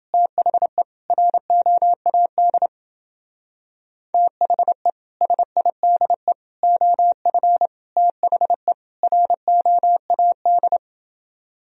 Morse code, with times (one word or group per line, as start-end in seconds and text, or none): 0.1-0.8 THE
1.1-2.7 ROAD
4.1-4.9 THE
5.2-6.3 SIDE
6.6-7.7 OF
8.0-8.7 THE
9.0-10.8 ROAD